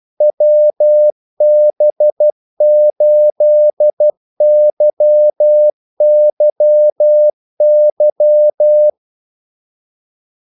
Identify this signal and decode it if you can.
Morse code, heard as a keyed tone, about 12 wpm, 600 Hz. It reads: WB8YYY
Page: 2 ekstra ordmellemrum mellem kaldesignalerne